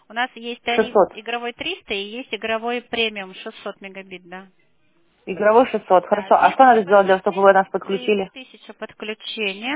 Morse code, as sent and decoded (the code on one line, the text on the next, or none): none